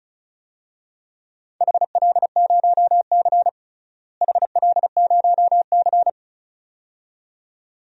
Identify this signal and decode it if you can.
Morse code, heard as a keyed tone, about 34 wpm, 700 Hz. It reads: HL0C HL0C
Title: DXCC-100 Call Signs - 4 Characters - Repeated 2x - 35wpm